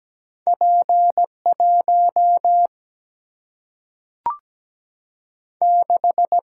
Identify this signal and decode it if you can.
Morse code, heard as a keyed tone, about 17 wpm, 700 Hz. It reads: P1 6
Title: Single-Letter Single-Number 17wpm